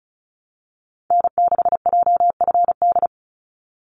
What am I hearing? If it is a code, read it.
Morse code, heard as a keyed tone, about 36 wpm, 700 Hz. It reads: N6JFD
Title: NA SOTA Activator Callsigns 35wpm